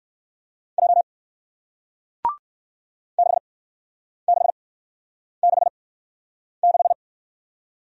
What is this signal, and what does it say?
Morse code, heard as a keyed tone, about 56 wpm, spaced as for 9 wpm, 700 Hz. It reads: J 6666
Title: Speed-Racing Single Letter-Number 45wpm